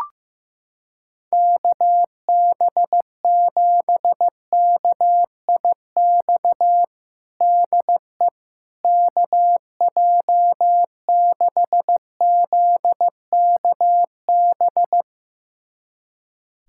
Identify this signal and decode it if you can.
Morse code, heard as a keyed tone, about 15 wpm, 700 Hz. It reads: KB7KIX DE KJ6ZKB
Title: USA Callsign DE Callsign 15wpm